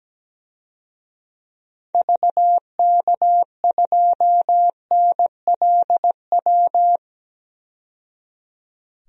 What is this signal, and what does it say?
Morse code, heard as a keyed tone, about 17 wpm, 700 Hz. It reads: VK2NLW
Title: DXCC-100 Call Signs - 6 Characters - 17wpm